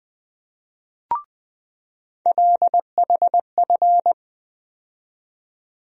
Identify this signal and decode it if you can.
Morse code, heard as a keyed tone, about 20 wpm, 700 Hz. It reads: LHF
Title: USA Call Sign Postfixes - 3 Letters 20wpm